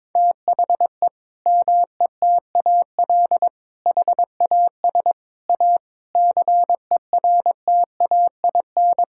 Morse code, heard as a keyed tone, about 22 wpm, 700 Hz. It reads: THE METAL HAS A CERTAIN
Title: Sentences from Top 1000 Words - Review Part 6 - 22wpm